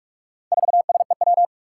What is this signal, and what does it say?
Morse code, heard as a keyed tone, about 46 wpm, 700 Hz. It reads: 4SEW
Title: Four Challenging Letter-Number Combinations 45wpm